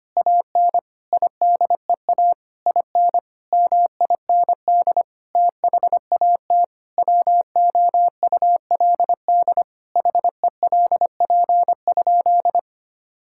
Morse code, heard as a keyed tone, about 25 wpm, 700 Hz. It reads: AN IDEA IN MIND THAT WOULD HELP?
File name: practice / 25wpm / Sentences with 9-Word Phrases - 25wpm